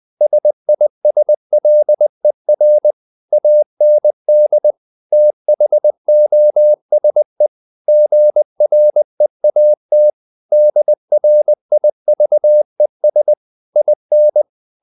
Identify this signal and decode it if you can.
Morse code, heard as a keyed tone, about 20 wpm, 600 Hz. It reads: SISLER AND THOSE GREAT DRIVES IN
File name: the old man and the sea / the old man and the sea part 1 20wpm